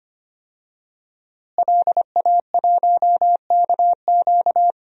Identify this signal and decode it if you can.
Morse code, heard as a keyed tone, about 25 wpm, 700 Hz. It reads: LA1KQ
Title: DXCC-100 Call Signs - 5 Characters - Repeated 3x - 25wpm